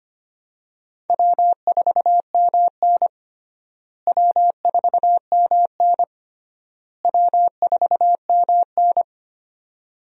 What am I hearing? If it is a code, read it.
Morse code, heard as a keyed tone, about 25 wpm, 700 Hz. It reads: W4MN W4MN W4MN
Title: USA Call Signs - 4 Characters - Repeated 3x - 25wpm